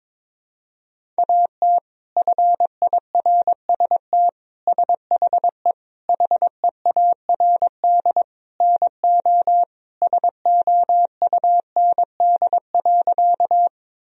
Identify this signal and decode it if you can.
Morse code, heard as a keyed tone, about 22 wpm, 700 Hz. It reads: AT FIRST SHE HEARD NO SOUND.
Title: Sentences from Top 1000 Words - Review Part 1 - 22wpm